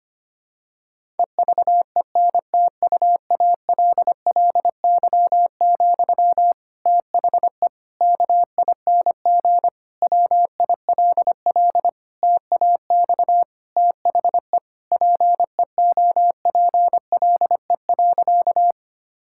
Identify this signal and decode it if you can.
Morse code, heard as a keyed tone, about 25 wpm, 700 Hz. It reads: EVENTUALLY, THE KING WILL TAX THE PEOPLE.
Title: Sentences from Top 1000 Words - Review Part 6 - 25wpm